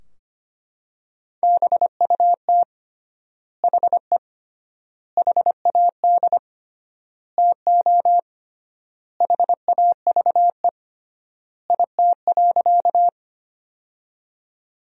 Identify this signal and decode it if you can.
Morse code, heard as a keyed tone, about 25 wpm, 700 Hz. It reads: BUT HE HAD TO HAVE IT.
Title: 3x Word-Spacing - Sentences from Top 100 Words - 25wpm